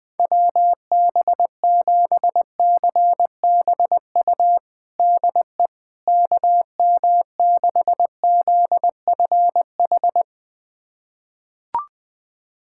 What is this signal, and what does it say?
Morse code, heard as a keyed tone, about 20 wpm, 700 Hz. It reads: WB7CBU DE KM6ZFH
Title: USA Callsign DE Callsign - 6 Characters - 20wpm